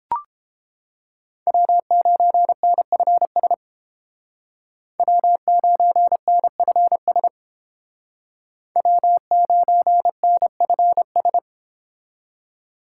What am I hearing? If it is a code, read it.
Morse code, heard as a keyed tone, about 31 wpm, 700 Hz. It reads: W9NFS W9NFS W9NFS